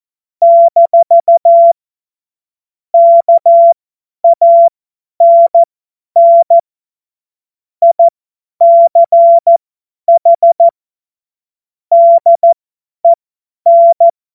Morse code, heard as a keyed tone, about 14 wpm, spaced as for 10 wpm, 680 Hz. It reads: - KANN ICH DEN
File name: Morsekurs / Witze / Witz 13 Mai 2025 50 F70